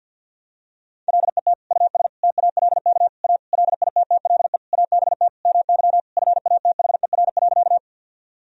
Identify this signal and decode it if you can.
Morse code, heard as a keyed tone, about 51 wpm, 700 Hz. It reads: LET US TALK A LITTLE ABT MY FATHER.
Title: Sentences from Top 400 Words and Abbreviations 50wpm